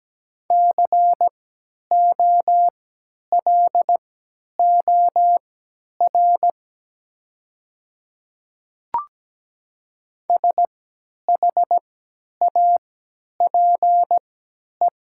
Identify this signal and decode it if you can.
Morse code, heard as a keyed tone, about 17 wpm, spaced as for 10 wpm, 700 Hz. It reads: COLOR SHAPE
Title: 3x Character Spacing - Top 500 Words - 17wpm